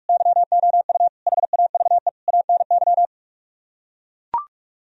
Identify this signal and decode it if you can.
Morse code, heard as a keyed tone, about 45 wpm, 700 Hz. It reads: YOU HAVE ANY